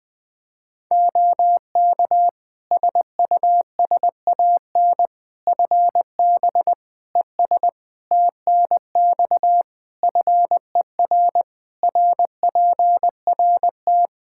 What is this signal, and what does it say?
Morse code, heard as a keyed tone, about 20 wpm, 700 Hz. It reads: OK SUSAN FB ES TNX FER RPRT
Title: Practice QSO - Exchange 2 - Info - 20wpm